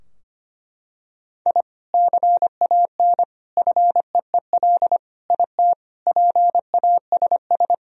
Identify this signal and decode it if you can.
Morse code, heard as a keyed tone, about 25 wpm, 700 Hz. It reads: I CAN FEEL IT PASS
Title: Sentences from Top 1000 Words - Review Part 2 - 25wpm